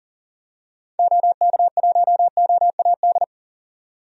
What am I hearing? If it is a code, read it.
Morse code, heard as a keyed tone, about 38 wpm, 700 Hz. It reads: OK1OAD
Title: DXCC-100 Call Signs - 6 Characters - 40wpm